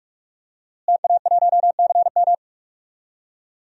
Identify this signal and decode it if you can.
Morse code, heard as a keyed tone, about 46 wpm, 700 Hz. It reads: TA1CM
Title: DXCC-100 Call Signs - 5 Characters - Repeated 2x - 45wpm